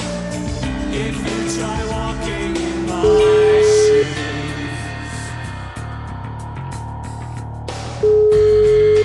Ringback tone (Europe, 425 Hz), with noise.